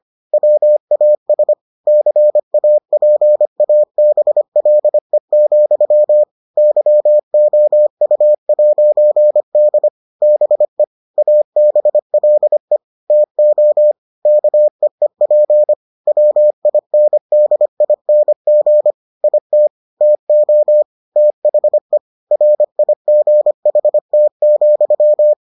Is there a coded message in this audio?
Yes: Morse code from 0.3 to 25.3 s, heard as a keyed tone, about 25 wpm, 600 Hz. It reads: WAS CAPABLE, YOU'D BE ABLE TO KEEP WINDING IT TO THE RIGHT,